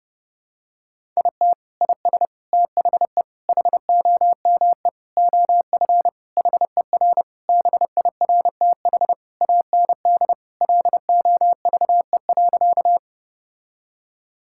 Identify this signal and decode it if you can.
Morse code, heard as a keyed tone, about 32 wpm, 700 Hz. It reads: IT IS THE HOME OF HER BIRTH AND LOVE.
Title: Sentences from Top 1000 Words - Review Part 5 - 30wpm